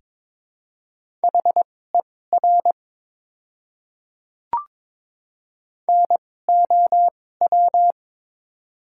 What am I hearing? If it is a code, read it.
Morse code, heard as a keyed tone, about 21 wpm, spaced as for 16 wpm, 700 Hz. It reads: HER NOW
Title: 2x Character Spacing - Top 300 Words - 22wpm